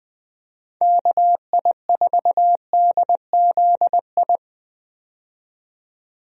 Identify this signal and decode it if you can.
Morse code, heard as a keyed tone, about 20 wpm, 700 Hz. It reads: KI4DZI
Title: USA Call Signs - 6 Characters - 20wpm